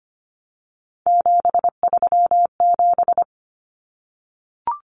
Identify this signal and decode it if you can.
Morse code, heard as a keyed tone, about 25 wpm, 700 Hz. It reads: 737
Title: Numbers - 3-Digit 25wpm